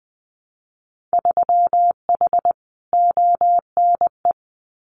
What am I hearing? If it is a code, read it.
Morse code, heard as a keyed tone, about 20 wpm, 700 Hz. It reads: 3H ONE
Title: Winter Field Day Exchanges Encore 20wpm